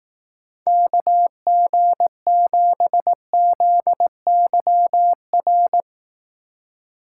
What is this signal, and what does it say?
Morse code, heard as a keyed tone, about 18 wpm, 700 Hz. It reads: KG7ZYR
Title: Speed Racing - USA 2x3 Call Signs 15wpm